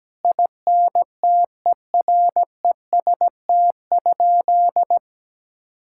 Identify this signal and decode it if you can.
Morse code, heard as a keyed tone, about 17 wpm, 700 Hz. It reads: INTEREST?
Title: Sentences from Top 1000 Words - Review Part 5 - 17wpm